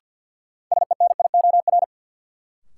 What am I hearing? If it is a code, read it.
Morse code, heard as a keyed tone, about 47 wpm, 700 Hz. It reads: SENIOR
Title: Sets of 5 Words Encore - New Word Drill - 50wpm